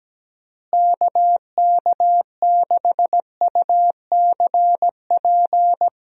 Morse code, heard as a keyed tone, about 17 wpm, 700 Hz. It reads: KK6UCP
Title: USA Call Signs - 6 Characters - Repeated 2x - 17wpm